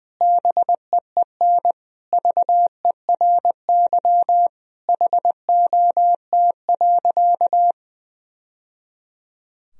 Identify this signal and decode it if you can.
Morse code, heard as a keyed tone, about 20 wpm, 700 Hz. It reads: BEEN VERY HOT.
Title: Sentences from Top 1000 Words - Review Part 3 - 20wpm